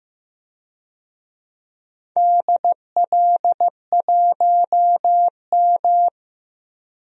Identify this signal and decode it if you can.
Morse code, heard as a keyed tone, about 15 wpm, 700 Hz. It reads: DL1M
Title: DXCC-100 Call Signs - 4 Characters - 15wpm